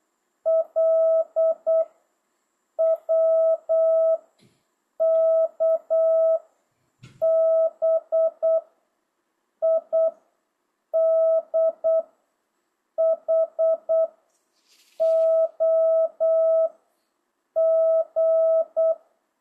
Morse code, heard as a keyed tone, about 8 wpm, 660 Hz. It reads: L W K B I D H O G